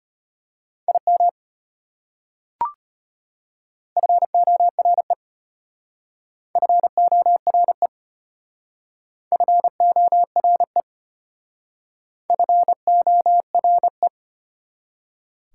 Morse code, heard as a keyed tone, about 32 wpm, 700 Hz. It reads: IM FORE FORE FORE FORE